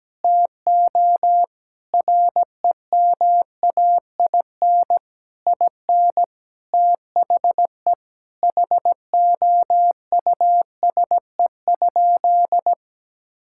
Morse code, heard as a keyed tone, about 17 wpm, 700 Hz. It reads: TO REMAIN IN THE HOUSE?
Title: Sentences from Top 1600 Words 17wpm